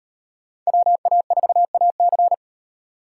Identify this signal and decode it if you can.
Morse code, heard as a keyed tone, about 38 wpm, 700 Hz. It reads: WA4AC